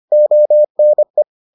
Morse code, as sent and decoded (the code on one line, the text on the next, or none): --- -. .
ONE